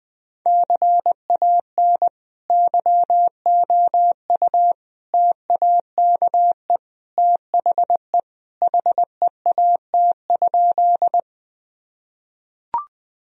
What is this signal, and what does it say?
Morse code, heard as a keyed tone, about 21 wpm, 700 Hz. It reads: CAN YOU TAKE THE HEAT?
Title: Sentences from Top 1000 Words - Review Part 3 - 20wpm